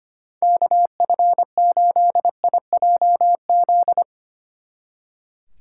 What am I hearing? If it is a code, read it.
Morse code, heard as a keyed tone, about 32 wpm, 700 Hz. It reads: KF8IJZ